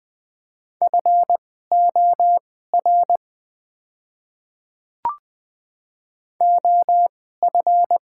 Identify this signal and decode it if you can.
Morse code, heard as a keyed tone, about 21 wpm, spaced as for 15 wpm, 700 Hz. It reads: FOR OF
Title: 2x Character Spacing - Top 300 Words - 20wpm